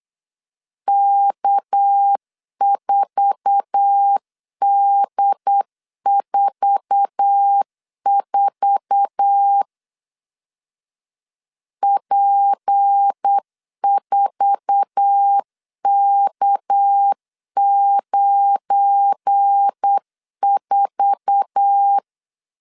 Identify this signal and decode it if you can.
Morse code, heard as a keyed tone, about 8 wpm, 800 Hz. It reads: K4D44 P4K94
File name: LEZIONI / 8 YJ45 / 8 YJ4 8wpm 3246